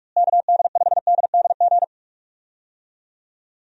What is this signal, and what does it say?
Morse code, heard as a keyed tone, about 46 wpm, 700 Hz. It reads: KD5DDG
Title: USA Call Signs - 6 Characters - Repeated 2x - 45wpm